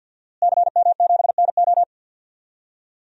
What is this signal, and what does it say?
Morse code, heard as a keyed tone, about 52 wpm, 700 Hz. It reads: CM7NO
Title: DXCC-100 Call Signs - 5 Characters - Repeated 3x - 50wpm